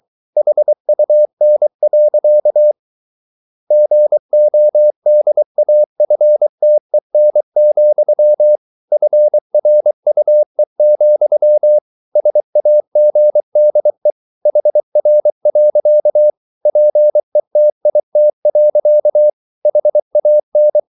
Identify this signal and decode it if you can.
Morse code, heard as a keyed tone, about 24 wpm, 600 Hz. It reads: HUN. GODAFTEN, FRUE, SAGDE HR. PETIT. HAN
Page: Ferie 23 wpm